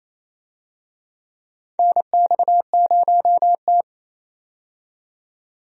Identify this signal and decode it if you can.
Morse code, heard as a keyed tone, about 28 wpm, 700 Hz. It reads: NX0T
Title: USA Call Signs - 4 Characters - 28wpm